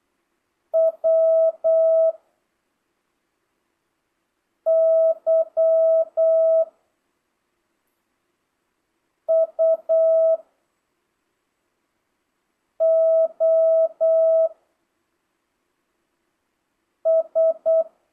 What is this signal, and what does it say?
Morse code, heard as a keyed tone, about 8 wpm, 650 Hz. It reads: W Y U O S
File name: jh8cbh / 2025cw